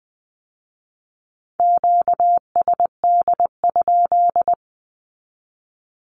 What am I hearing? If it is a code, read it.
Morse code, heard as a keyed tone, about 20 wpm, 700 Hz. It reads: QSD?
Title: Q-Codes with Spoken Meaning - 20wpm